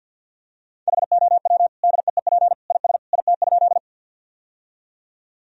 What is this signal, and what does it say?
Morse code, heard as a keyed tone, about 50 wpm, 700 Hz. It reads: HOW DEEP IS IT?
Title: Sentences from Top 1000 Words - Review Part 3 - 50wpm